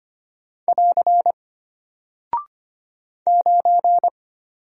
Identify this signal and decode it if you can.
Morse code, heard as a keyed tone, about 25 wpm, 700 Hz. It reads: <AR> 9
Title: Single Character (Letter:Number:Punctuation:Prosign) Alt 25wpm